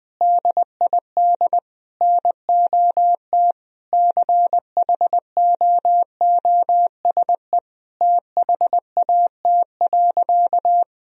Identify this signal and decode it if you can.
Morse code, heard as a keyed tone, about 20 wpm, 700 Hz. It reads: DID NOT CHOOSE THAT.
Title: Sentences from Top 1000 Words - Review Part 4 - 20wpm